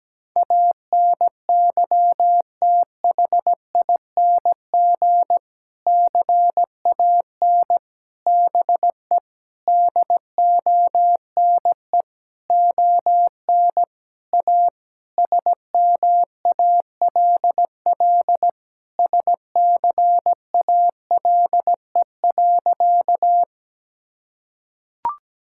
Morse code, heard as a keyed tone, about 17 wpm, 700 Hz. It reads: ANYTHING CAN BE DONE ON A SMALL SCALE.
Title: Sentences from Top 1000 Words - Review Part 5 - 17wpm